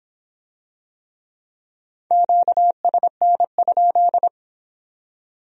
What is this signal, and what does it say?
Morse code, heard as a keyed tone, about 24 wpm, 700 Hz. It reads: QSN?